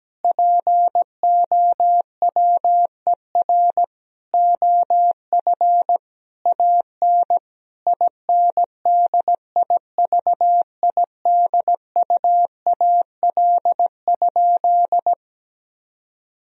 Morse code, heard as a keyed tone, about 17 wpm, 700 Hz. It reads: POWER OF AN INDIVIDUAL?
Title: Sentences from Top 1000 Words - Review Part 4 - 17wpm